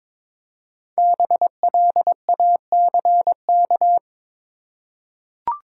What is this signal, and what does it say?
Morse code, heard as a keyed tone, about 22 wpm, 690 Hz. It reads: BLACK
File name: 5 Letter Words - First 3 Letters Given - 22wpm